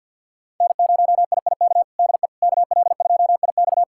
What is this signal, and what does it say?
Morse code, heard as a keyed tone, about 50 wpm, 700 Hz. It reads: N0IIK DE KL2IX
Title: USA Callsign DE Callsign - 5 Characters - 50wpm